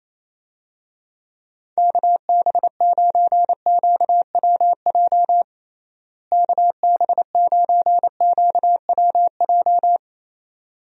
Morse code, heard as a keyed tone, about 27 wpm, 700 Hz. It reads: KB9QWJ KB9QWJ
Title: USA Call Signs - 6 Characters - Repeated 2x - 28wpm